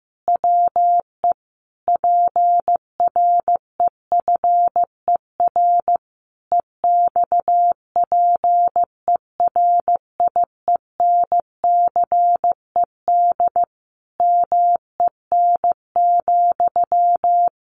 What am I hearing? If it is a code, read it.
Morse code, heard as a keyed tone, about 15 wpm, 700 Hz. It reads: WE PREFER EXPERIENCED MEN,
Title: Sentences from Top 1100 Words 15wpm